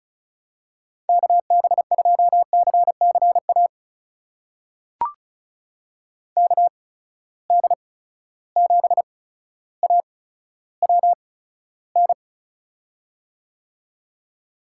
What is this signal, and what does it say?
Morse code, heard as a keyed tone, about 36 wpm, spaced as for 15 wpm, 700 Hz. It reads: KB2CCA KD7AWN
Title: USA Call Signs - 8x Character Spacing - 35wpm